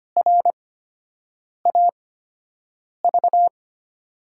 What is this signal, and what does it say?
Morse code, heard as a keyed tone, about 25 wpm, spaced as for 7 wpm, 700 Hz. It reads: RAV
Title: USA Call Signs - 6 Characters - 8x Character Spacing - 25wpm